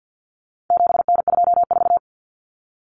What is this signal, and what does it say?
Morse code, heard as a keyed tone, about 50 wpm, 700 Hz. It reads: 7N24
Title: Common Cut Numbers - 4-Digit 50wpm